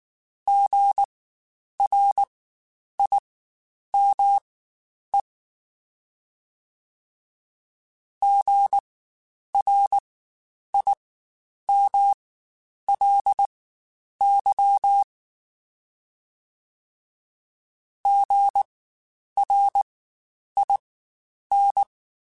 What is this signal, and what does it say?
Morse code, heard as a keyed tone, about 19 wpm, spaced as for 9 wpm, 800 Hz. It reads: GRIME GRIMLY GRIN
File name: Spaced Alphas / Spaced Alphas mp3s / Gs sp @18